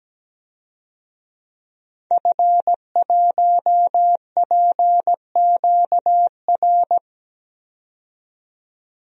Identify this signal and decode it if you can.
Morse code, heard as a keyed tone, about 17 wpm, 700 Hz. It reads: F1PQR
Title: DXCC-100 Call Signs - 5 Characters - 17wpm